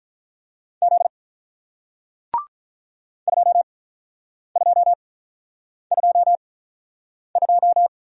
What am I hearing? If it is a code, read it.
Morse code, heard as a keyed tone, about 48 wpm, spaced as for 9 wpm, 690 Hz. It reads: Z 2222